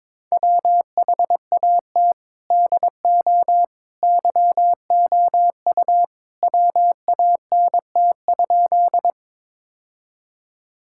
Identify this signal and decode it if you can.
Morse code, heard as a keyed tone, about 22 wpm, 700 Hz. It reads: WHAT DO YOU WANT?
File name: practice / 22wpm / Sentences with 2-Word Phrases - 22wpm